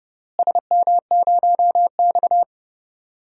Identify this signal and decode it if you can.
Morse code, heard as a keyed tone, about 30 wpm, 700 Hz. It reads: SM0X
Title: DXCC-100 Call Signs - 4 Characters - Repeated 2x - 30wpm